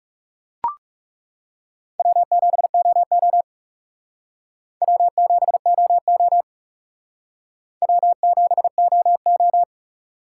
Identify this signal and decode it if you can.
Morse code, heard as a keyed tone, about 38 wpm, 700 Hz. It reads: W7OO W7OO W7OO